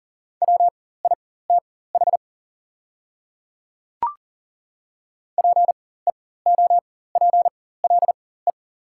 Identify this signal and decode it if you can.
Morse code, heard as a keyed tone, about 38 wpm, spaced as for 18 wpm, 690 Hz. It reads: WITH PEOPLE
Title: 4x Character Spacing - Top 100 Words - 40wpm